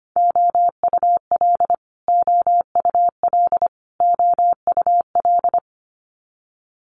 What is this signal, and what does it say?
Morse code, heard as a keyed tone, about 25 wpm, 700 Hz. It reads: OUL OUL OUL
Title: Multi-letter Phonemes Repeated 3x 25wpm